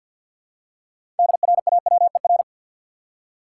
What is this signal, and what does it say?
Morse code, heard as a keyed tone, about 52 wpm, 700 Hz. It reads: DRAWER
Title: Six Letter Words - 50wpm